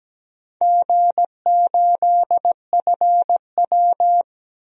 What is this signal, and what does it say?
Morse code, heard as a keyed tone, about 17 wpm, 700 Hz. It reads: G8FW
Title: DXCC-100 Call Signs - 4 Characters - 17wpm